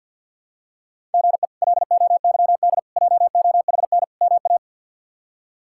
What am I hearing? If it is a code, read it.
Morse code, heard as a keyed tone, about 47 wpm, 700 Hz. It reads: GE LOYD JOHN MA
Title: K1USN SST QSO Practice - 50wpm